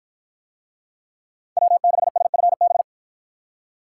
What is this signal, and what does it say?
Morse code, heard as a keyed tone, about 53 wpm, 700 Hz. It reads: W6SFB